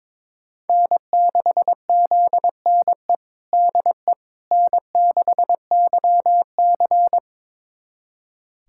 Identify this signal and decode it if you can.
Morse code, heard as a keyed tone, about 22 wpm, 700 Hz. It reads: N6ZNE DE N6YC
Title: USA Callsign DE Callsign 22wpm